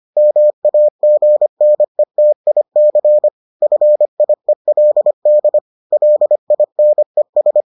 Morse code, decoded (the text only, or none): MAGNETIC FIELD LINES